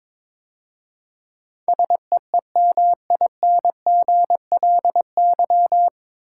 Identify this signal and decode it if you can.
Morse code, heard as a keyed tone, about 21 wpm, 700 Hz. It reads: SEEMINGLY
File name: 9 Letter Words - First 3 Letters Given - 22wpm